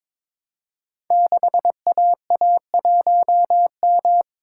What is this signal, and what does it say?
Morse code, heard as a keyed tone, about 22 wpm, 700 Hz. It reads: 6AA1M
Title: Code Groups - 5 Letter:Numbers - 22wpm